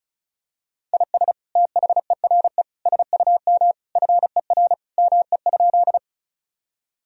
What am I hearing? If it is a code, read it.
Morse code, heard as a keyed tone, about 36 wpm, 700 Hz. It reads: IS THERE SUM FER ME?
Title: Sentences from Top 200 Words and Abbreviations 35wpm